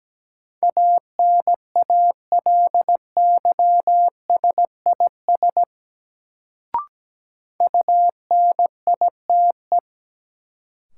Morse code, heard as a keyed tone, about 17 wpm, 700 Hz. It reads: ANALYSIS UNITE
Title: Sentences from Top 1000 Words - Review Part 5 - New Word Drill - 17wpm